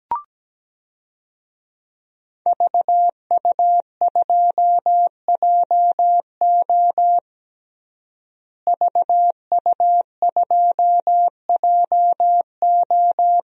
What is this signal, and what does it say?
Morse code, heard as a keyed tone, about 17 wpm, 700 Hz. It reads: VU2JO VU2JO
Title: DXCC-100 Call Signs - 5 Characters - Repeated 3x - 17wpm